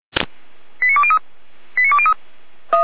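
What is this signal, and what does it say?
Five-tone selective call (ZVEI), heard as quick runs of five tones, tones of about 70 ms. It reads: 89203 89203
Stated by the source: ZVEI five-tone selective call 89315 received on SDR.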